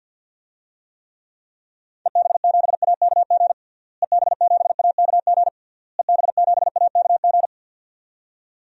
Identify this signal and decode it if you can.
Morse code, heard as a keyed tone, about 47 wpm, 700 Hz. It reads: EB7AKG EB7AKG EB7AKG
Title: DXCC-100 Call Signs - 6 Characters - Repeated 3x - 50wpm